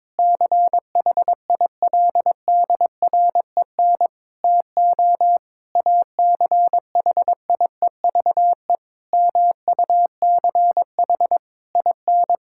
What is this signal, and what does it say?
Morse code, heard as a keyed tone, about 22 wpm, 700 Hz. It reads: CHILDREN TO ACHIEVE MUCH IN
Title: Sentences from Top 1000 Words - Review Part 6 - 22wpm